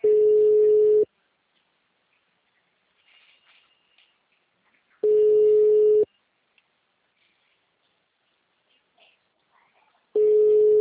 Ringback tone (Europe, 425 Hz).